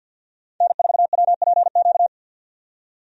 Morse code, heard as a keyed tone, about 53 wpm, 700 Hz. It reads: N4WPQ
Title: USA Call Signs - 5 Characters - 50wpm